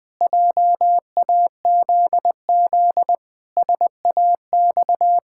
Morse code, heard as a keyed tone, about 21 wpm, 700 Hz. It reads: JAZZ SAX
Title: Pangram Challenge v2 - 20wpm